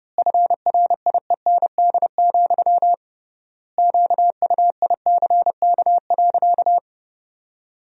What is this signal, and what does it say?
Morse code, heard as a keyed tone, about 30 wpm, 700 Hz. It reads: FRIEND, QUICK.
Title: 3x Word-Spacing - Sentences from Top 500 Words - 30wpm